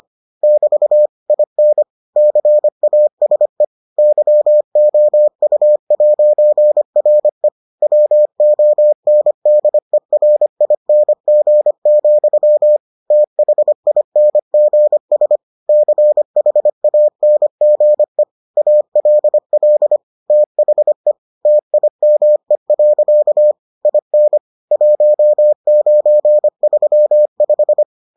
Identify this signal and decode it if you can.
Morse code, heard as a keyed tone, about 25 wpm, 600 Hz. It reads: <BT> IN CASE YOU'RE WONDERING, THINGS CHANGE ALL THE TIME. IN 1935